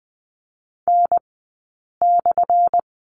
Morse code, heard as a keyed tone, about 20 wpm, spaced as for 8 wpm, 700 Hz. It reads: N/
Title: 190 Lesson 31d - Letter v - Two Characters 20wpm